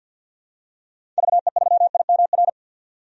Morse code, heard as a keyed tone, about 52 wpm, 700 Hz. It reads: VE3IMR
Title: DXCC-100 Call Signs - 6 Characters - 50wpm